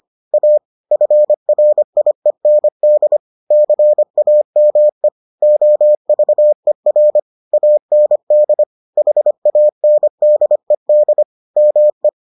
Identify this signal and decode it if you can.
Morse code, heard as a keyed tone, about 25 wpm, 600 Hz. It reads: A FRIEND CAME OVER AND HANDED ME